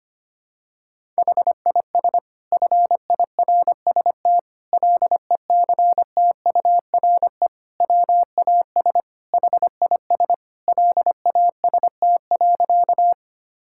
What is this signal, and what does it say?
Morse code, heard as a keyed tone, about 25 wpm, 700 Hz. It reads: HIS FIRST LECTURE WAS HIS LAST.
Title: Sentences from Top 1600 Words 25wpm